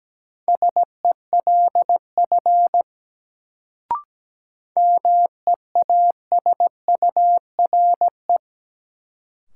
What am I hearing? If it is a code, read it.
Morse code, heard as a keyed tone, about 17 wpm, 700 Hz. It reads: SELF MEASURE
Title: Sentences from Top 1000 Words - Review Part 1 - New Word Drill - 17wpm